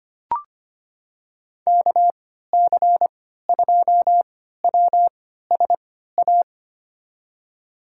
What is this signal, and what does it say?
Morse code, heard as a keyed tone, about 25 wpm, spaced as for 14 wpm, 700 Hz. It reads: KC2WSA
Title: USA Call Signs - 3x Character Spacing - 25wpm